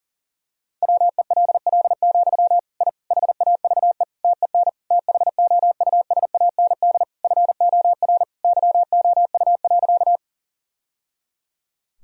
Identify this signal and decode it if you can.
Morse code, heard as a keyed tone, about 40 wpm, 700 Hz. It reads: WELL, I HAVE TEN THOUSAND FOR YOU.
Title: Sentences from Top 1000 Words - Review Part 3 - 40wpm